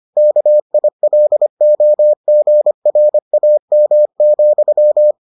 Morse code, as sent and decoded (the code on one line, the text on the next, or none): -.- .. .-.. --- --. .-. .- -- --..--
KILOGRAM,